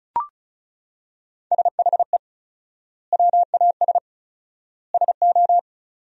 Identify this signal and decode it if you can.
Morse code, heard as a keyed tone, about 36 wpm, 700 Hz. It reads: SHE WAS SO
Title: 4x Word-Spacing - Sentences from Top 400 Words - 35wpm